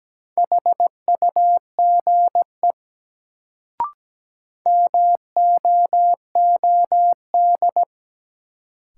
Morse code, heard as a keyed tone, about 17 wpm, 700 Hz. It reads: HUGE MOOD
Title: Sentences from Top 1100 Words - New Word Drill - 17wpm